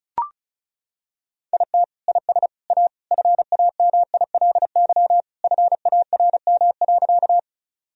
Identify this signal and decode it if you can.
Morse code, heard as a keyed tone, about 34 wpm, 700 Hz. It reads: IT IS A FAMILY FARM.
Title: Sentences from Top 1000 Words - Review Part 2 - 35wpm